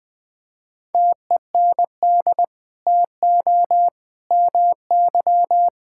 Morse code, heard as a keyed tone, about 21 wpm, 700 Hz. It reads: TEND TO MY
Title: Sentences from Top 1000 Words - Review Part 5 - 20wpm